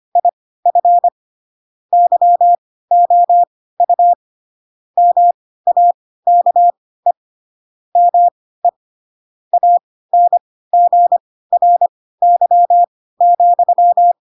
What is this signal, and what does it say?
Morse code, heard as a keyed tone, about 25 wpm, spaced as for 16 wpm, 700 Hz. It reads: IF YOU MAKE ME ANGRY,
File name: Pinocchio / Pinocchio ch4 10 25 ik1hge